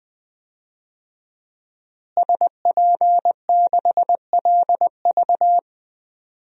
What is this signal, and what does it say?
Morse code, heard as a keyed tone, about 20 wpm, 700 Hz. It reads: SP6LV
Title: DXCC-100 Call Signs - 5 Characters - Repeated 2x - 20wpm